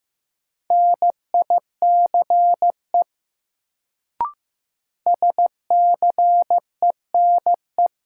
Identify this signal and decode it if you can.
Morse code, heard as a keyed tone, about 15 wpm, 700 Hz. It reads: NICE SCENE
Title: Sentences from Top 1100 Words - New Word Drill Alt - 15wpm